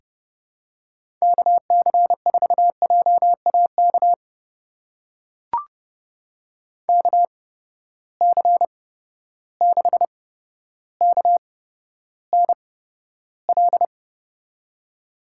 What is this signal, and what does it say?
Morse code, heard as a keyed tone, about 30 wpm, spaced as for 13 wpm, 700 Hz. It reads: KC4JAK KC6KNL
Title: USA Call Signs - 6 Characters - 8x Character Spacing - 30wpm